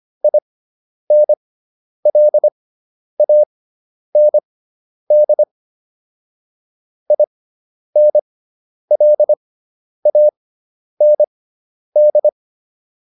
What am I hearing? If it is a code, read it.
Morse code, heard as a keyed tone, about 25 wpm, spaced as for 10 wpm, 600 Hz. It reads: INLAND INLAND